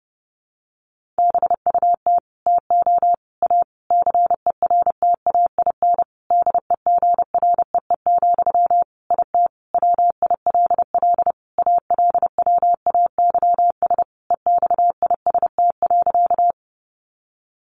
Morse code, heard as a keyed tone, about 30 wpm, 700 Hz. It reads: BUT TO A CERTAIN DEGREE, IT WILL ALWAYS EXIST.
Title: Sentences from Top 1500 Words 30wpm